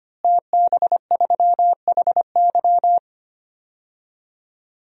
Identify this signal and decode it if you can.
Morse code, heard as a keyed tone, about 25 wpm, 700 Hz. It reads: TB3HY
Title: DXCC-100 Call Signs - 5 Characters - Repeated 3x - 25wpm